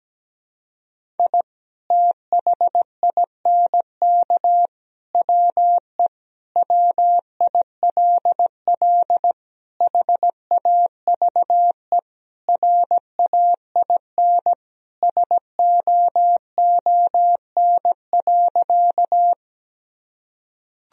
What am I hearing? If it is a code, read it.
Morse code, heard as a keyed tone, about 17 wpm, 700 Hz. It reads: I THINK WE WILL HAVE RAIN SOON.
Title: Sentences from Top 1000 Words - Review Part 1 - 17wpm